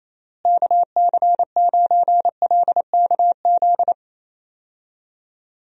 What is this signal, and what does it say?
Morse code, heard as a keyed tone, about 27 wpm, 700 Hz. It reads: KC9LKZ
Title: USA Call Signs - 6 Characters - Repeated 2x - 28wpm